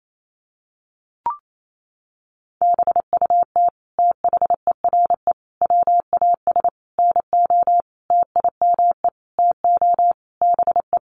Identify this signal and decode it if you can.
Morse code, heard as a keyed tone, about 28 wpm, 700 Hz. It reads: BUT THERE WAS NO TIME TO BE
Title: Sentences from Top 1400 Words 28wpm